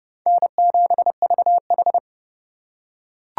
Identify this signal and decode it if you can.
Morse code, heard as a keyed tone, about 30 wpm, 700 Hz. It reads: N7VH
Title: USA Call Signs - 4 Characters - 4x Character Spacing - 30wpm